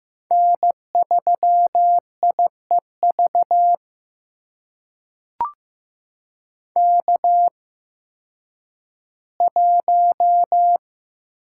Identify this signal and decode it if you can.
Morse code, heard as a keyed tone, about 15 wpm, spaced as for 10 wpm, 700 Hz. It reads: N3IEV K1
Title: USA Call Signs - 8x Character Spacing - 15wpm